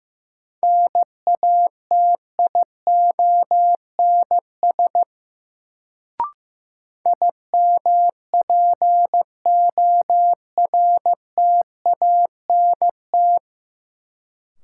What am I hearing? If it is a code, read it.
Morse code, heard as a keyed tone, about 15 wpm, 700 Hz. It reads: NATIONS IMPORTANT
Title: Sets of 3 Words - New Word Drill - 15wpm